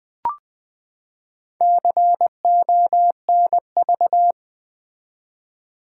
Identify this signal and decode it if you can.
Morse code, heard as a keyed tone, about 20 wpm, 700 Hz. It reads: CONV
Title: Build A Word - 4 Letters At A Time - 10-Letter Words - 20wpm